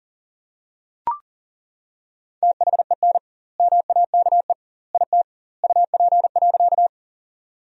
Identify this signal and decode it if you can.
Morse code, heard as a keyed tone, about 40 wpm, 700 Hz. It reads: THEN MAKE IT UP.
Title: 2x Word-Spacing - Sentences from Top 100 Words - 40wpm